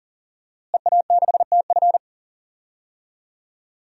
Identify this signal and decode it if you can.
Morse code, heard as a keyed tone, about 40 wpm, 700 Hz. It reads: EA6TF
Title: DXCC-100 Call Signs - 5 Characters - Repeated 3x - 40wpm